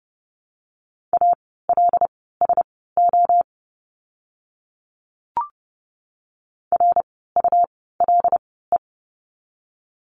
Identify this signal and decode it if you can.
Morse code, heard as a keyed tone, about 30 wpm, spaced as for 17 wpm, 700 Hz. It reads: ALSO RULE
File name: Lesson 17f - Letter u - Words 30wpm